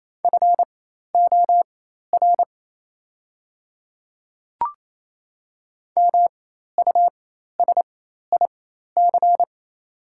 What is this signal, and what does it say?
Morse code, heard as a keyed tone, about 28 wpm, spaced as for 13 wpm, 700 Hz. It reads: FOR MUSIC
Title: 4x Character Spacing - Top 300 Words - 28wpm